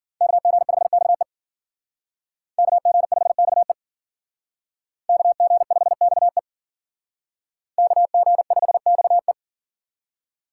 Morse code, heard as a keyed tone, about 50 wpm, 700 Hz. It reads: KG5XE KG5XE KG5XE KG5XE